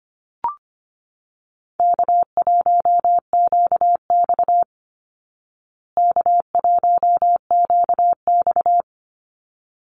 Morse code, heard as a keyed tone, about 25 wpm, 700 Hz. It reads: K1QX K1QX
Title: Easy Call Signs Repeated 2x - 25wpm